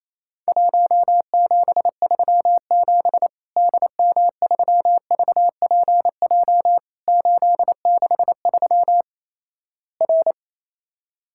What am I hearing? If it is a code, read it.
Morse code, heard as a keyed tone, about 28 wpm, 700 Hz. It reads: 1737 DM3VPJ 863 R
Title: Worked All Europe DX Contest - QTC Exchange - 28wpm